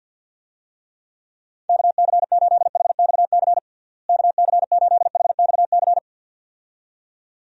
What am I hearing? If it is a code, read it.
Morse code, heard as a keyed tone, about 48 wpm, 700 Hz. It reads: KC8HXC KC8HXC
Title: USA Call Signs - 6 Characters - Repeated 2x - 50wpm